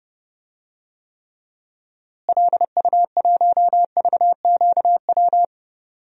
Morse code, heard as a keyed tone, about 30 wpm, 700 Hz. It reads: LU1VQW
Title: DXCC-100 Call Signs - 6 Characters - Repeated 2x - 30wpm